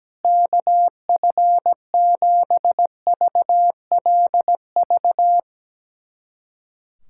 Morse code, heard as a keyed tone, about 17 wpm, 700 Hz. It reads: KF7VLV
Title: USA Callsign DE Callsign 17wpm